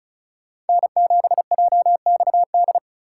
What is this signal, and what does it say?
Morse code, heard as a keyed tone, about 44 wpm, 700 Hz. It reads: N7JXD